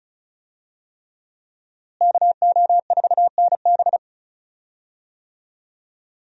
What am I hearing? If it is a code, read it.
Morse code, heard as a keyed tone, about 36 wpm, 700 Hz. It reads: KO4NB
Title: USA Call Signs - 5 Characters - Repeated 3x - 35wpm